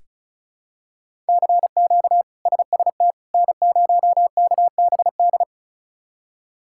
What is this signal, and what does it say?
Morse code, heard as a keyed tone, about 34 wpm, 700 Hz. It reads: CQ SST N0KBD
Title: K1USN SST QSO Practice - 35wpm